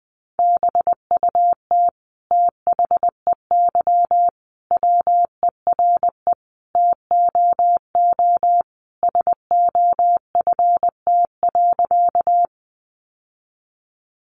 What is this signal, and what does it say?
Morse code, heard as a keyed tone, about 20 wpm, 700 Hz. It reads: BUT THEY WERE TOO SOFT.
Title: Sentences from Top 1500 Words 20wpm